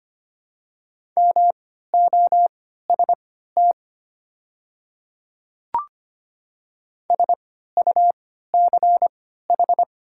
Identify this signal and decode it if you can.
Morse code, heard as a keyed tone, about 25 wpm, spaced as for 14 wpm, 700 Hz. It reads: MOST SUCH
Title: 3x Character Spacing - Top 200 Words - 25wpm